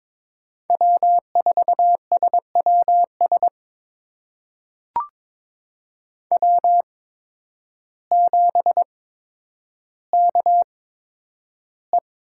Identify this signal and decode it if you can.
Morse code, heard as a keyed tone, about 22 wpm, spaced as for 10 wpm, 700 Hz. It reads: W4SWS W7KE
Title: USA Call Signs - 5 Characters - 8x Character Spacing - 22wpm